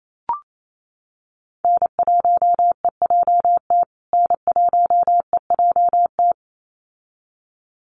Morse code, heard as a keyed tone, about 28 wpm, 700 Hz. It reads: N1EJT N1EJT
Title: USA Call Signs Repeated 2x 28wpm